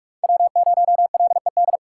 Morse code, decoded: W0LED

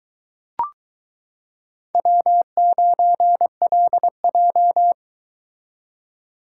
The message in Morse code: W9LJ

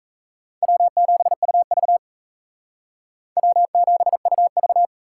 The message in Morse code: W7UV W7UV

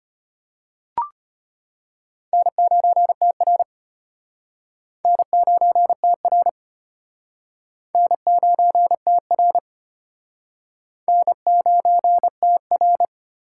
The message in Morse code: N9TR N9TR N9TR N9TR

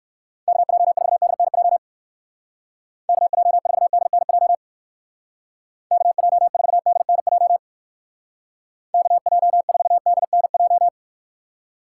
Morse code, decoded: KJ4DNJ KJ4DNJ KJ4DNJ KJ4DNJ

